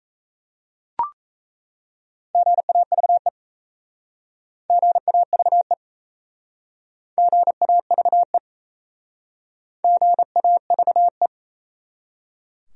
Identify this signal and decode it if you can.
Morse code, heard as a keyed tone, about 37 wpm, 700 Hz. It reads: GAVE GAVE GAVE GAVE